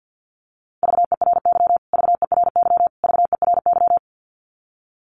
Morse code, heard as a keyed tone, about 49 wpm, 700 Hz. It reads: VERY VERY VERY
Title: Top 200 Words Repeated 50wpm